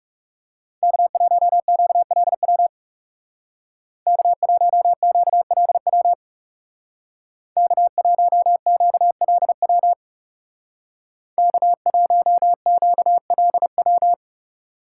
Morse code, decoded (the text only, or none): K1QLW K1QLW K1QLW K1QLW